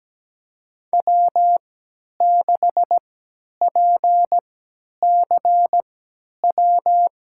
Morse code, heard as a keyed tone, about 17 wpm, spaced as for 10 wpm, 700 Hz. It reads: W6PCW
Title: USA Call Signs - 5 Characters - 3x Character Spacing - 17wpm